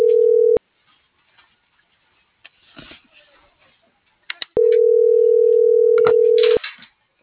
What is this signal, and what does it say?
Ringback tone (North America), 440 + 480 Hz.